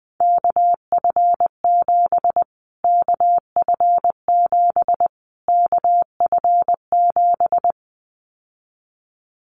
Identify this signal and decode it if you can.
Morse code, heard as a keyed tone, about 20 wpm, 700 Hz. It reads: KF7 KF7 KF7
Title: Callsign Prefixes for Top 25 DXCC - Repeated 3x - 20wpm